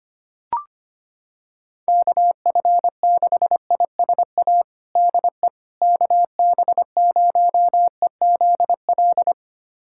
Morse code, heard as a keyed tone, about 25 wpm, 700 Hz. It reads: KF6ISA DE KB0EZL
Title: USA Callsign DE Callsign - 6 Characters - 25wpm